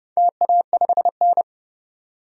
Morse code, heard as a keyed tone, about 30 wpm, 700 Hz. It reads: TA5N
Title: DXCC-100 Call Signs - 4 Characters - 30wpm